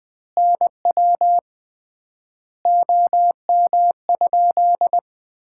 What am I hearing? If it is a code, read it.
Morse code, heard as a keyed tone, about 20 wpm, 700 Hz. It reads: NW OM?
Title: Common QSO Phrases - 3x Word-Spacing - 20wpm